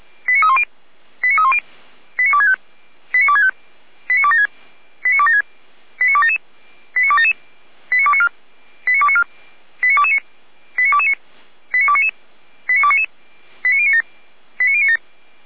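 Five-tone selective call (ZVEI), heard as quick runs of five tones, tones of about 70 ms. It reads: 89310 89310 89265 89265 89276 89276 89277 89277 89294 89294 89229 89229 89299 89299 89097 89097
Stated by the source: Five-tone ZVEI selective call received on SDR, feuerwehr-orlamuende.org fire brigade audio server.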